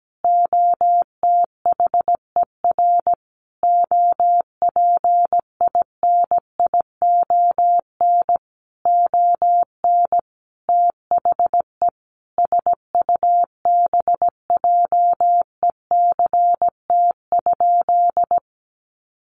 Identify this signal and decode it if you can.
Morse code, heard as a keyed tone, about 17 wpm, 700 Hz. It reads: OTHER OPINION ON THE SUBJECT?
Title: Sentences from Top 1200 Words 17wpm